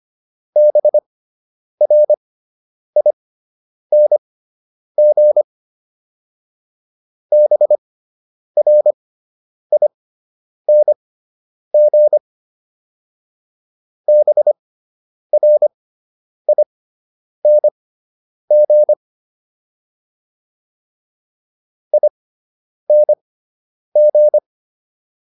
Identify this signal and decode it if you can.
Morse code, heard as a keyed tone, about 25 wpm, spaced as for 9 wpm, 600 Hz. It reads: BRING BRING BRING ING